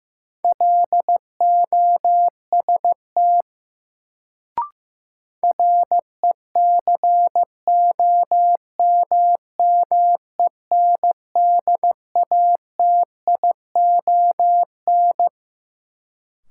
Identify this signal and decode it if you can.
Morse code, heard as a keyed tone, about 15 wpm, 700 Hz. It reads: LOST RECOMMENDATION
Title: Sentences from Top 1400 Words - New Word Drill - 15wpm